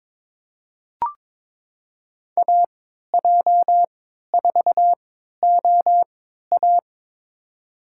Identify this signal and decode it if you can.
Morse code, heard as a keyed tone, about 22 wpm, spaced as for 13 wpm, 700 Hz. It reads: AJ4OA
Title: USA Call Signs - 3x Character Spacing - 22wpm